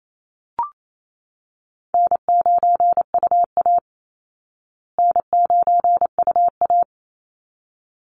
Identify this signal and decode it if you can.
Morse code, heard as a keyed tone, about 28 wpm, 700 Hz. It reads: N9UA N9UA
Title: Easy Call Signs Repeated 2x - 28wpm